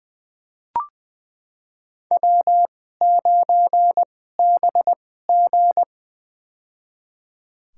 Morse code, heard as a keyed tone, about 20 wpm, spaced as for 14 wpm, 700 Hz. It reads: W9BG
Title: USA Call Signs - 4 Characters - 2x Character Spacing - 20wpm